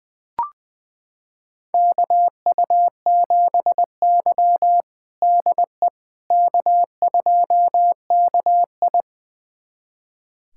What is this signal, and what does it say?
Morse code, heard as a keyed tone, about 20 wpm, 700 Hz. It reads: KU7Y DE K2KI
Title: USA Callsign DE Callsign - 4 Characters - 20wpm